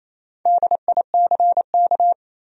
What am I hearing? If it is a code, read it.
Morse code, heard as a keyed tone, about 28 wpm, 700 Hz. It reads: DICK
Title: USA Common Names - Max 5 Letters - 28wpm